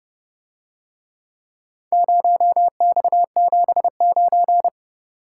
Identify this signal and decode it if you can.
Morse code, heard as a keyed tone, about 28 wpm, 700 Hz. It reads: 0X79